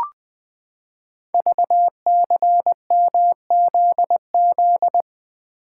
Morse code, heard as a keyed tone, about 20 wpm, 700 Hz. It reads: VCMZZ
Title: Code Groups - 5 Letters - 20wpm